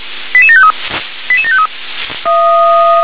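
Five-tone selective call (ZVEI), heard as quick runs of five tones, tones of about 70 ms. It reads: 88753 88753
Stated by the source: ZVEI five-tone selective call 88726, received by SDR.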